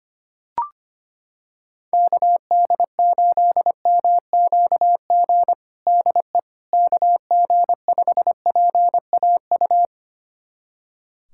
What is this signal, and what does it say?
Morse code, heard as a keyed tone, about 25 wpm, 700 Hz. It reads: KD8MQG DE KG5PAU
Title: USA Callsign DE Callsign 25wpm